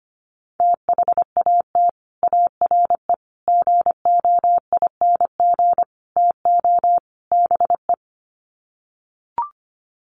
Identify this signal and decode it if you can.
Morse code, heard as a keyed tone, about 25 wpm, 700 Hz. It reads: THAT ARE GOING TO BE
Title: Sets of 5 Words Encore2 25wpm